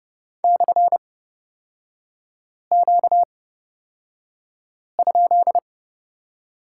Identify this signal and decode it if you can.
Morse code, heard as a keyed tone, about 29 wpm, spaced as for 5 wpm, 700 Hz. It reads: /Q?
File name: Lesson 38e - Letter q - Three Characters 30wpm